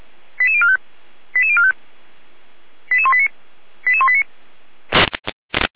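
Five-tone selective call (ZVEI), heard as quick runs of five tones, tones of about 70 ms. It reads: 88045 88045 88189 88189
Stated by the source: ZVEI five-tone call received by SDR.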